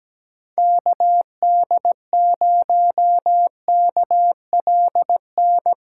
Morse code, heard as a keyed tone, about 17 wpm, 700 Hz. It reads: KD0KLN
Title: USA Callsign DE Callsign - 6 Characters - 17wpm